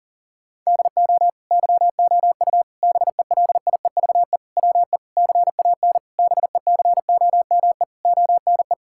Morse code, heard as a keyed tone, about 40 wpm, 700 Hz. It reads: DO YOU BELIEVE WE CAN BECOME ONE
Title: Sentences from Top 1000 Words - Review Part 6 - 40wpm